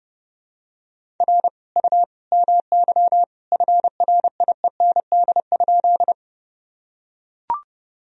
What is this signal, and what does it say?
Morse code, heard as a keyed tone, about 32 wpm, 700 Hz. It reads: R U MY FRIEND?
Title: Sentences from Top 300 Words and Abbreviations 30wpm